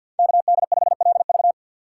Morse code, heard as a keyed tone, about 51 wpm, 700 Hz. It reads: KD5LV
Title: USA Call Signs - 5 Characters - 3x Character Spacing - 50wpm